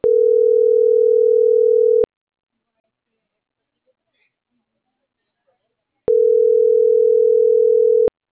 Ringback tone (North America), 440 + 480 Hz.